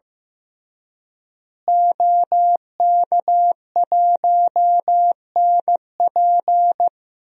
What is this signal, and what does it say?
Morse code, heard as a keyed tone, about 15 wpm, 700 Hz. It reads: OK1NP
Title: DXCC-100 Call Signs - 5 Characters - Repeated 3x - 15wpm